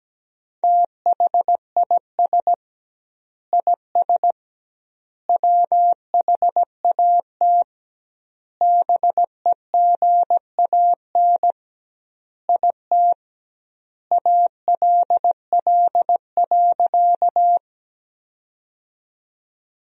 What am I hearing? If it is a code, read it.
Morse code, heard as a keyed tone, about 17 wpm, 700 Hz. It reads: THIS IS WHAT BEGAN IT ALL.
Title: 2x Word-Spacing - Sentences from Top 300 Words - 17wpm